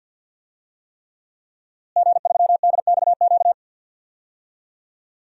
Morse code, heard as a keyed tone, about 48 wpm, 700 Hz. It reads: G3DXQ
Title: DXCC-100 Call Signs - 5 Characters - Repeated 3x - 50wpm